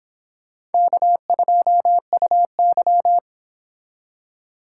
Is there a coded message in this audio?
Yes: Morse code from 0.7 to 3.2 s, heard as a keyed tone, about 26 wpm, 700 Hz. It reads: K2UY